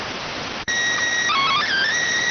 SSTV (slow-scan television), PD 290.